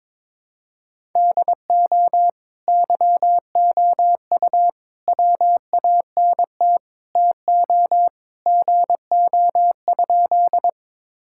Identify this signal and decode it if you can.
Morse code, heard as a keyed tone, about 22 wpm, 700 Hz. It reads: DO YOU WANT TO GO?
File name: practice / 22wpm / Sentences with 2-Word Phrases - 22wpm